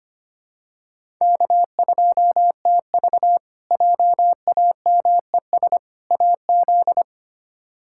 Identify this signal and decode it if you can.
Morse code, heard as a keyed tone, about 25 wpm, 700 Hz. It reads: K2TV JAMES AZ
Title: K1USN SST QSO Practice - 25wpm